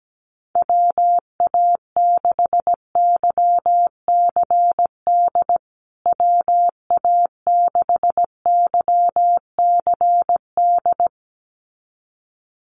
Morse code, heard as a keyed tone, about 17 wpm, 700 Hz. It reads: WA6YCD WA6YCD
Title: USA Call Signs Repeated 2x 17wpm